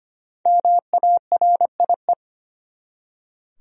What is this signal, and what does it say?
Morse code, heard as a keyed tone, about 25 wpm, 700 Hz. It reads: MARIE
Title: USA Common Names - 25wpm